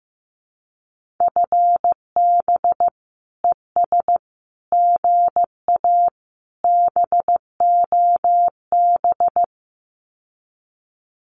Morse code, heard as a keyed tone, about 15 wpm, 700 Hz. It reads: FB ES GA BOB
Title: Common QSO Phrases Encore 15wpm